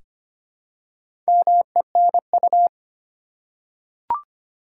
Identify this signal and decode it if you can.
Morse code, heard as a keyed tone, about 25 wpm, 700 Hz. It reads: MENU
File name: practice / 25wpm / Easy Words with Hints - 25wpm